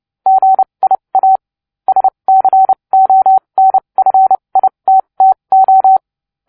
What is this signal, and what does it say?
Morse code, heard as a keyed tone, about 30 wpm, spaced as for 24 wpm, 760 Hz. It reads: GIA SCONFITTO